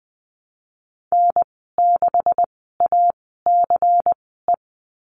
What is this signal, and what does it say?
Morse code, heard as a keyed tone, about 20 wpm, spaced as for 15 wpm, 700 Hz. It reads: N6ACE
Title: 128 Lesson 21g - Number 6 - Call Signs 20wpm